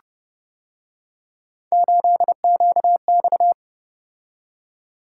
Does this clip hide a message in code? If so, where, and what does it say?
Yes: Morse code from 1.7 to 3.5 s, heard as a keyed tone, about 29 wpm, 700 Hz. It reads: 8QX